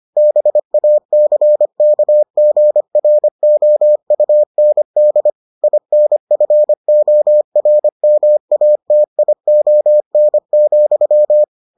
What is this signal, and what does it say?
Morse code, heard as a keyed tone, about 25 wpm, 600 Hz. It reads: BACKGROUND INFORMATION,